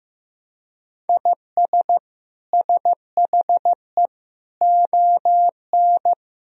Morse code, heard as a keyed tone, about 15 wpm, 700 Hz. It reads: IS SHE ON
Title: Sentences from Top 1000 Words - Review Part 3 - 15wpm